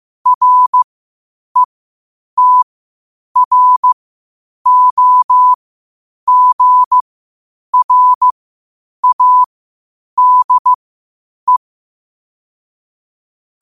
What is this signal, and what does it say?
Morse code, heard as a keyed tone, about 15 wpm, spaced as for 8 wpm, 1000 Hz. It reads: RETROGRADE
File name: qotdfiles / 8wpm